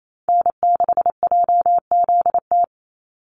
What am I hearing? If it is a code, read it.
Morse code, heard as a keyed tone, about 28 wpm, 700 Hz. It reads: N6JZT
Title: NA SOTA Chaser Callsigns 28wpm